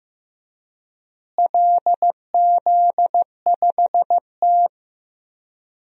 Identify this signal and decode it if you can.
Morse code, heard as a keyed tone, about 15 wpm, 700 Hz. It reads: LZ5T
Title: DXCC-100 Call Signs - 4 Characters - Repeated 2x - 15wpm